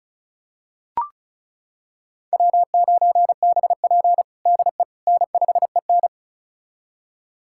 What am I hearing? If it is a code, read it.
Morse code, heard as a keyed tone, about 36 wpm, 700 Hz. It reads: W9BP DE N5EN
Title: USA Callsign DE Callsign - 4 Characters - 35wpm